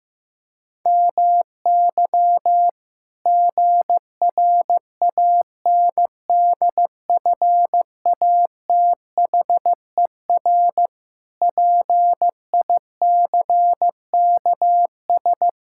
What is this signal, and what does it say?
Morse code, heard as a keyed tone, about 16 wpm, 700 Hz. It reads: MY GRANDFATHER PICKS
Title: Pangram Challenge v2 - 15wpm